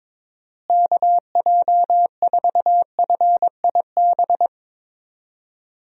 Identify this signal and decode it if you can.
Morse code, heard as a keyed tone, about 22 wpm, 700 Hz. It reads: KJ4FIB
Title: USA Call Signs - 6 Characters - Repeated 3x - 22wpm